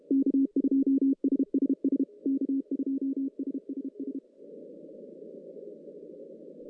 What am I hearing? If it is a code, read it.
Morse code, heard as a keyed tone, about 32 wpm, 280 Hz. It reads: K2SSS K2SSS
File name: 2015 cq ww cw / 2015 cqwww k5tr - 0-41